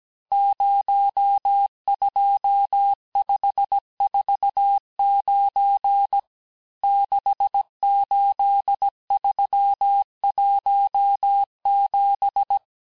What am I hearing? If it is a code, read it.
Morse code, heard as a keyed tone, about 17 wpm, 800 Hz. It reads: 02549 68317
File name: corrige-cours-1581-texte-chiffre-vitesse-16-mpm